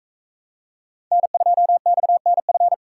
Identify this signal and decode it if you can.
Morse code, heard as a keyed tone, about 41 wpm, 700 Hz. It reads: N2XNF